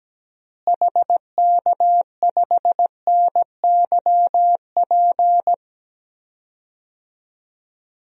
Morse code, heard as a keyed tone, about 17 wpm, 700 Hz. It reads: HK5NYP
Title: DXCC-100 Call Signs - 6 Characters - Repeated 2x - 17wpm